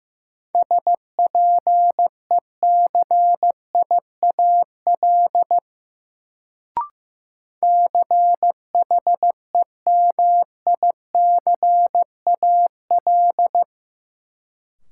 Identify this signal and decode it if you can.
Morse code, heard as a keyed tone, about 15 wpm, 700 Hz. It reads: SPECIAL CHEMICAL
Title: Sentences from Top 1200 Words - New Word Drill Alt - 15wpm